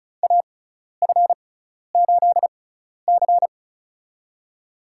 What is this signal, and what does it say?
Morse code, heard as a keyed tone, about 34 wpm, spaced as for 12 wpm, 700 Hz. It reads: AF8C
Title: USA Call Signs - 4 Characters - 6x Character Spacing - 35wpm